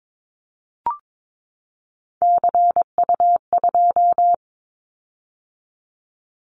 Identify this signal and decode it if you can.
Morse code, heard as a keyed tone, about 22 wpm, 700 Hz. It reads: CU2
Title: Callsign Prefixes for Top 50 DXCC - 22wpm